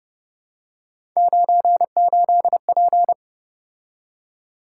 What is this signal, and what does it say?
Morse code, heard as a keyed tone, about 30 wpm, 700 Hz. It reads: 98P
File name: Speed-Racing - Three Letter-Number Combinations - 25wpm